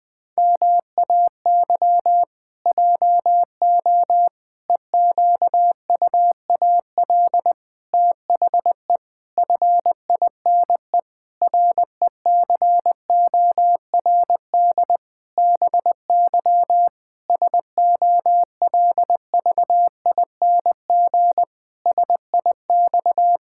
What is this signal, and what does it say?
Morse code, heard as a keyed tone, about 20 wpm, 700 Hz. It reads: MAY JO EQUAL THE FINE RECORD BY SOLVING SIX
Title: Pangram Challenge v2 - 20wpm